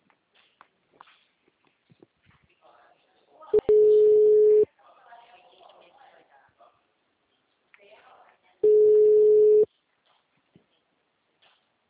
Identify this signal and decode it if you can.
Ringback tone (Europe, 425 Hz), with speech.